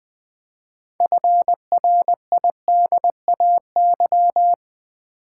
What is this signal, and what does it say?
Morse code, heard as a keyed tone, about 20 wpm, 700 Hz. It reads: FRIDAY
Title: 6 Letter Words - First 3 Letters Given - 20wpm